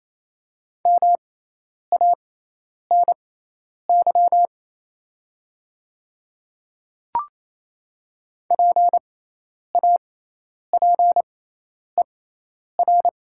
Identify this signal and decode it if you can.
Morse code, heard as a keyed tone, about 28 wpm, spaced as for 10 wpm, 690 Hz. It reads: MANY PAPER
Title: 6x Character Spacing - Top 300 Words - 28wpm